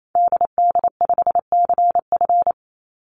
Morse code, heard as a keyed tone, about 28 wpm, 700 Hz. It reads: DD5CF
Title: Call Signs - 28wpm